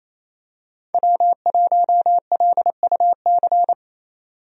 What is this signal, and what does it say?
Morse code, heard as a keyed tone, about 28 wpm, 700 Hz. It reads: W1LUC
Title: USA Call Signs - 5 Characters - 28wpm